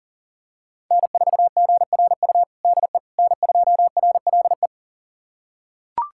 Morse code, heard as a keyed tone, about 40 wpm, 700 Hz. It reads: N4GRU DE N2RLE
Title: USA Callsign DE Callsign - 5 Characters - 40wpm